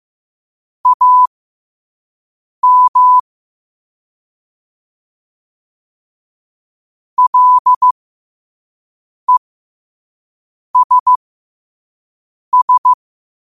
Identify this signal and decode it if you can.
Morse code, heard as a keyed tone, about 15 wpm, spaced as for 5 wpm, 1000 Hz. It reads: AM LESS
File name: qotdfiles / 5wpm